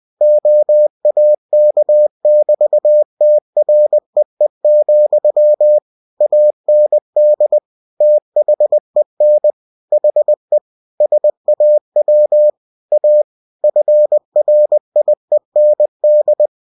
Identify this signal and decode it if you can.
Morse code, heard as a keyed tone, about 20 wpm, 600 Hz. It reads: OAK<BT>TREE, AND THEN HE SAW A FRIEND
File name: cw-books / winnie-the-pooh / winnie-20wpm-page-0024